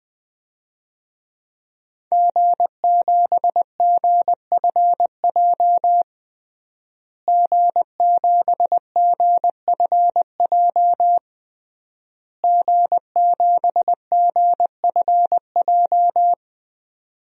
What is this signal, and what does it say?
Morse code, heard as a keyed tone, about 20 wpm, 700 Hz. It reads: G7GFJ G7GFJ G7GFJ
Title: DXCC-100 Call Signs - 5 Characters - Repeated 3x - 20wpm